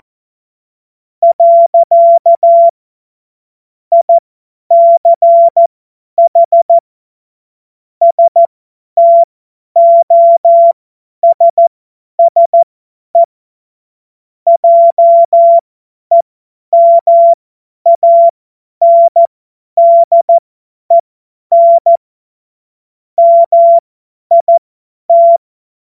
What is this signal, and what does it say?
Morse code, heard as a keyed tone, about 14 wpm, spaced as for 10 wpm, 680 Hz. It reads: . ICH STOSSE JEMANDEN MIT
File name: Morsekurs / Witze / Witz 12. April 2022 50 F70